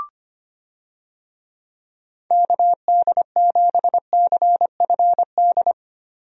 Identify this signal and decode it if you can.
Morse code, heard as a keyed tone, about 25 wpm, 700 Hz. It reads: KD7CFD
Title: USA Call Signs - 6 Characters - Repeated 3x - 25wpm